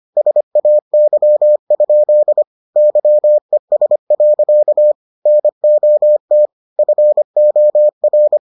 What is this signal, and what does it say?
Morse code, heard as a keyed tone, about 25 wpm, 600 Hz. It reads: SAY? YES. NOT FOR